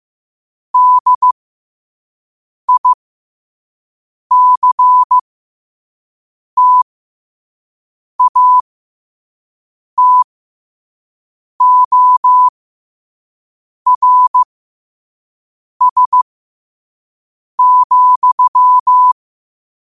Morse code, heard as a keyed tone, about 15 wpm, spaced as for 5 wpm, 1000 Hz. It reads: DICTATORS,